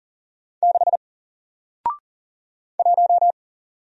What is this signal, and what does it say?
Morse code, heard as a keyed tone, about 40 wpm, 700 Hz. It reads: 6 1
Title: Single Letter-Number Alt 40wpm